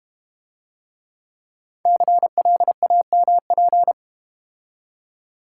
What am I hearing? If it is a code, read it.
Morse code, heard as a keyed tone, about 32 wpm, 700 Hz. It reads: CLAMP